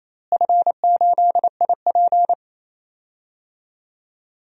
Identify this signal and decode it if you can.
Morse code, heard as a keyed tone, about 28 wpm, 700 Hz. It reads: F8IP
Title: DXCC-100 Call Signs - 4 Characters - Repeated 2x - 28wpm